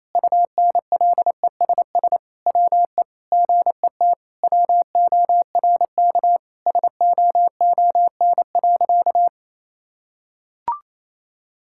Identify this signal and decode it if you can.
Morse code, heard as a keyed tone, about 28 wpm, 700 Hz. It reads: UNLESS WE GET WORK SOON.
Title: Sentences from Top 1000 Words - Review Part 6 - 28wpm